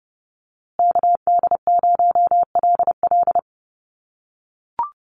Morse code, heard as a keyed tone, about 29 wpm, 700 Hz. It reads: KD0LL
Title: USA Call Signs Repeated 3x 30wpm